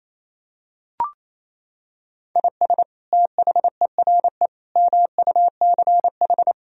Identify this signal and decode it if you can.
Morse code, heard as a keyed tone, about 28 wpm, 700 Hz. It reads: IS THERE MUCH
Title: Sentences from Top 1000 Words - Review Part 4 - 28wpm